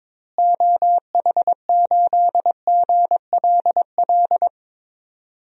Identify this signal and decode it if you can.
Morse code, heard as a keyed tone, about 22 wpm, 700 Hz. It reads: OH8GLL
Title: DXCC-100 Call Signs - 6 Characters - Repeated 3x - 22wpm